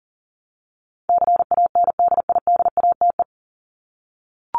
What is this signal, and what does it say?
Morse code, heard as a keyed tone, about 40 wpm, 700 Hz. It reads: CANDIDATE
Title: Top 1501-1750 Words 40wpm